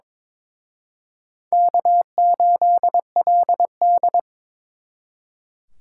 Morse code, heard as a keyed tone, about 26 wpm, 700 Hz. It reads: K8LD
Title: Speed Racing - USA 1x2 Call Signs 22wpm